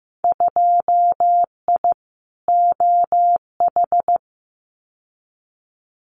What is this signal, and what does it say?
Morse code, heard as a keyed tone, about 15 wpm, 700 Hz. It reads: 2I OH
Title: Winter Field Day Exchanges 15wpm